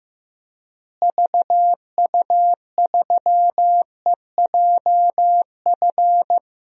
Morse code, heard as a keyed tone, about 15 wpm, 700 Hz. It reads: VU3EJF
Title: DXCC-100 Call Signs - 6 Characters - Repeated 2x - 15wpm